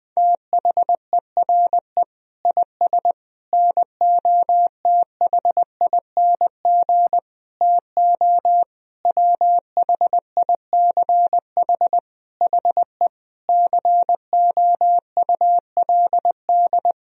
Morse code, heard as a keyed tone, about 20 wpm, 700 Hz. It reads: THERE IS NOTHING TO WHICH HE COULD
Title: Sentences from Top 1000 Words - Review Part 3 - 20wpm